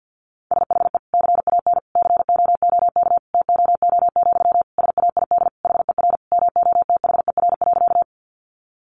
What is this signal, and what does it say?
Morse code, heard as a keyed tone, about 50 wpm, 700 Hz. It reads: SHE CAN COOK TOO, SAID HER MOTHER.
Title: Sentences from Top 700 Words 50wpm